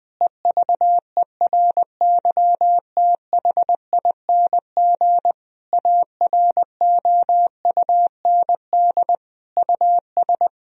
Morse code, heard as a keyed tone, about 20 wpm, 700 Hz. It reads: EVERYTHING AROUND US
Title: Sentences from Top 1000 Words - Review Part 3 - 20wpm